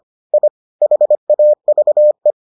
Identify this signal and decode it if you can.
Morse code, heard as a keyed tone, about 25 wpm, 600 Hz. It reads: I HAVE